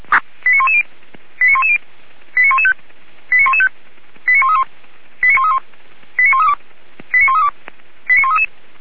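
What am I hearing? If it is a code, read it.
Five-tone selective call (ZVEI), heard as quick runs of five tones, tones of about 80 ms. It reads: 89110 89110 89115 89115 89131 89131 89132 89132 89133
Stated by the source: ZVEI five-tone selective call received by SDR.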